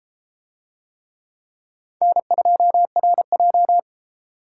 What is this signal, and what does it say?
Morse code, heard as a keyed tone, about 31 wpm, 700 Hz. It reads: N2RJ